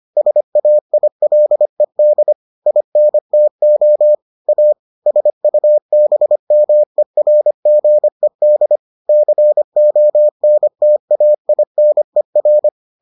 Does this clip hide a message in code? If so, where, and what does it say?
Yes: Morse code from 0.2 to 12.7 s, heard as a keyed tone, about 25 wpm, 600 Hz. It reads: SAILED INTO A SUBMERGED CONTAINER